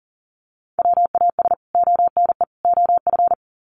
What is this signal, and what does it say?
Morse code, heard as a keyed tone, about 40 wpm, 700 Hz. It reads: WAS ONE OF
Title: Sets of 3 Words 40wpm